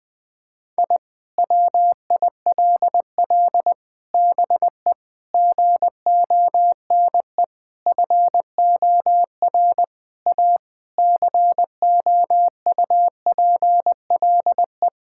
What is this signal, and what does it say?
Morse code, heard as a keyed tone, about 20 wpm, 700 Hz. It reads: I WILL BE GONE FOR A COUPLE
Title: Sentences with 4-Word Phrases - 20wpm